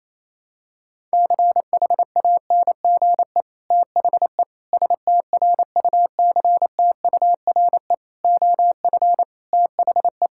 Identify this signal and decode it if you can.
Morse code, heard as a keyed tone, about 28 wpm, 700 Hz. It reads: CHANGE THE STRUCTURE OF THE
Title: Sentences from Top 1000 Words - Review Part 6 - 28wpm